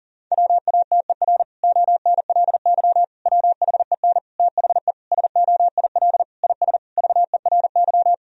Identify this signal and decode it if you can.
Morse code, heard as a keyed tone, about 37 wpm, 700 Hz. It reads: WATER ONLY WHEN THE SOIL IS VERY
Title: Sentences from Top 1000 Words - Review Part 4 - 40wpm